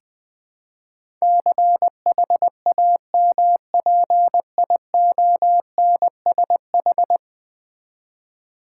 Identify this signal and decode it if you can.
Morse code, heard as a keyed tone, about 20 wpm, 700 Hz. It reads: CHAMPIONSH
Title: Build A Word - 5 Letters At A Time - 12-Letter Words - 20wpm